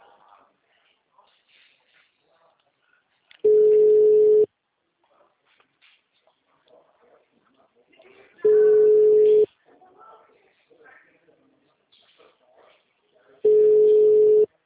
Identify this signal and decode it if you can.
Ringback tone (Europe, 425 Hz), with speech.